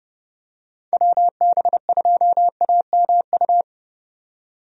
Morse code, heard as a keyed tone, about 30 wpm, 700 Hz. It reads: WB2AMU
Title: USA Call Signs - 6 Characters - 30wpm